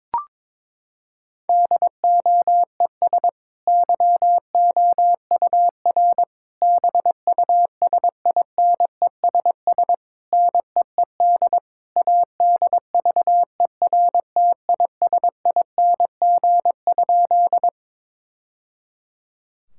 Morse code, heard as a keyed tone, about 21 wpm, 700 Hz. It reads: DOES YOUR BUSINESS NEED ADVERTISING?
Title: Sentences from Top 1000 Words - Review Part 6 - 22wpm